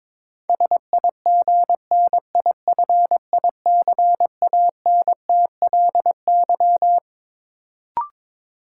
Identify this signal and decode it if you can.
Morse code, heard as a keyed tone, about 22 wpm, 700 Hz. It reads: SIGNIFICANTLY
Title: Sentences from Top 1500 Words - New Word Drill - 22wpm